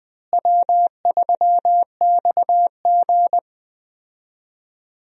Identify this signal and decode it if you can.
Morse code, heard as a keyed tone, about 20 wpm, 700 Hz. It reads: W3XG
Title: USA Call Signs - 4 Characters - Repeated 2x - 20wpm